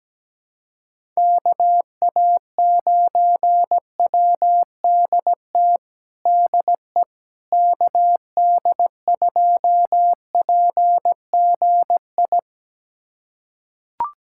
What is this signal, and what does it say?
Morse code, heard as a keyed tone, about 17 wpm, 700 Hz. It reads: KA9WDT DE KD2PGI
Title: USA Callsign DE Callsign - 6 Characters - 17wpm